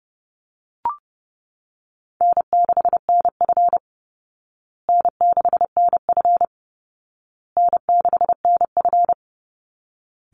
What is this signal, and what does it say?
Morse code, heard as a keyed tone, about 30 wpm, 700 Hz. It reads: N6NF N6NF N6NF
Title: Easy Call Signs Repeated 3x - 30wpm